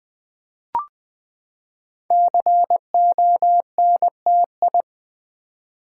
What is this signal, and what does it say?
Morse code, heard as a keyed tone, about 20 wpm, 700 Hz. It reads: CONTI
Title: Build A Word - 5 Letters At A Time - 10-Letter Words - 20wpm